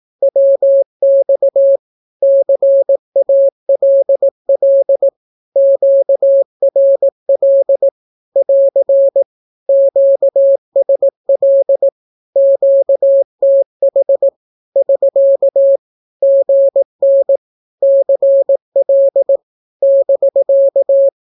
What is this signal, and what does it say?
Morse code, heard as a keyed tone, about 18 wpm, 550 Hz. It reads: WX CALL QRL <AR> QSL QTH <SK> GN CL <BK>
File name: naukapodstaw / skroty / skroty